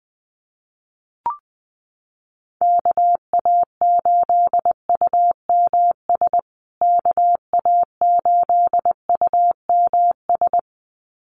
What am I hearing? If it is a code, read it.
Morse code, heard as a keyed tone, about 20 wpm, 700 Hz. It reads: KA8UMS KA8UMS
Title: USA Call Signs Repeated 2x 20wpm